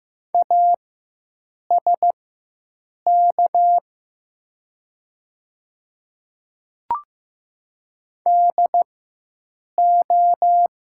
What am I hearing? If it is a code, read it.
Morse code, heard as a keyed tone, about 15 wpm, spaced as for 7 wpm, 700 Hz. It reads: ASK DO
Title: 4x Character Spacing - Top 200 Words - 15wpm